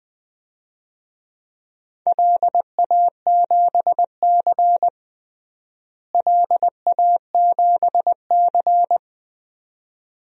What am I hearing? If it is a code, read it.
Morse code, heard as a keyed tone, about 21 wpm, 700 Hz. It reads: LA7C LA7C
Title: DXCC-100 Call Signs - 4 Characters - Repeated 2x - 20wpm